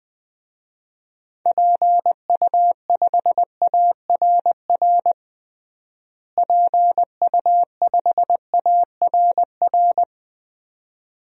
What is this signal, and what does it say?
Morse code, heard as a keyed tone, about 20 wpm, 700 Hz. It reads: PU5ARR PU5ARR
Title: DXCC-100 Call Signs - 6 Characters - Repeated 2x - 20wpm